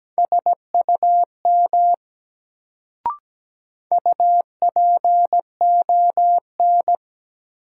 Morse code, heard as a keyed tone, about 17 wpm, 700 Hz. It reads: SUM UPON
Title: Sentences from Top 1000 Words - Review Part 6 - New Word Drill - 17wpm